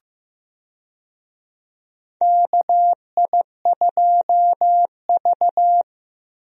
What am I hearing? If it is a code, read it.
Morse code, heard as a keyed tone, about 15 wpm, 700 Hz. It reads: KI2V
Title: USA Call Signs - 4 Characters - Repeated 3x - 15wpm